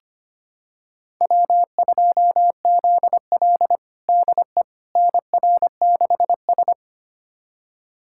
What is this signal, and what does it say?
Morse code, heard as a keyed tone, about 25 wpm, 700 Hz. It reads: W2ZL DE NR6S
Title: USA Callsign DE Callsign - 4 Characters - 25wpm